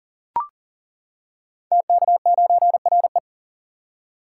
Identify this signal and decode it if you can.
Morse code, heard as a keyed tone, about 38 wpm, 700 Hz. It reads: TK9RE
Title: Code Groups - 5 Letter:Numbers - 40wpm